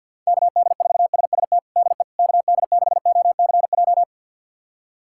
Morse code, heard as a keyed tone, about 46 wpm, 700 Hz. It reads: KD4SST DE KD6OCJ
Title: USA Callsign DE Callsign - 6 Characters - 50wpm